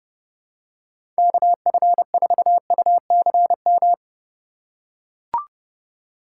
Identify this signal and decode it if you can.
Morse code, heard as a keyed tone, about 29 wpm, 700 Hz. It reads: KF4UCM
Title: USA Call Signs - 6 Characters - 2x Character Spacing - 30wpm